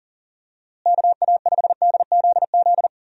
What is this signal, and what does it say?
Morse code, heard as a keyed tone, about 40 wpm, 700 Hz. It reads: KA5DZZ
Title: USA Call Signs - 6 Characters - Repeated 2x - 40wpm